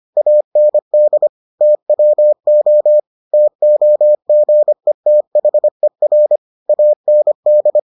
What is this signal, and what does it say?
Morse code, heard as a keyed tone, about 25 wpm, 600 Hz. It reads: AND TWO TOGETHER AND